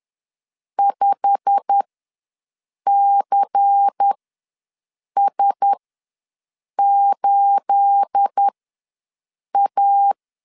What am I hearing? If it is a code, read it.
Morse code, heard as a keyed tone, about 10 wpm, spaced as for 6 wpm, 800 Hz. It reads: 5CS8A
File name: LEZIONI / 10 SIGNS / 10C DOT ws 10wpm 4213